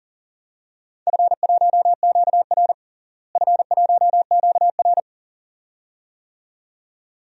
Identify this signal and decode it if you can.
Morse code, heard as a keyed tone, about 40 wpm, 700 Hz. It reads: F1QR F1QR
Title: DXCC-100 Call Signs - 4 Characters - Repeated 2x - 40wpm